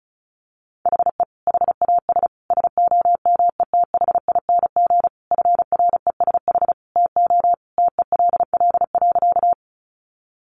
Morse code, heard as a keyed tone, about 34 wpm, 700 Hz. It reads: HE HAS SOMETHING FRESH TO TELL.
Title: Sentences from Top 1200 Words 35wpm